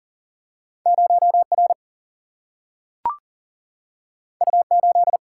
Morse code, heard as a keyed tone, about 40 wpm, 700 Hz. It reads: 0R U8
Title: Single-Letter Single-Number 40wpm